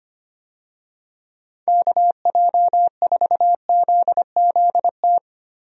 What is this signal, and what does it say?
Morse code, heard as a keyed tone, about 25 wpm, 700 Hz. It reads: KJ4ZZT
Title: USA Call Signs - 6 Characters - 25wpm